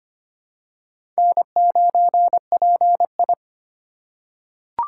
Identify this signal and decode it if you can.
Morse code, heard as a keyed tone, about 25 wpm, 700 Hz. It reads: N9PI
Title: USA Call Signs - 4x Character Spacing - 25wpm